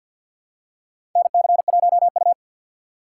Morse code, heard as a keyed tone, about 50 wpm, 700 Hz. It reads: NC1U
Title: USA Call Signs - 6x Character Spacing - 50wpm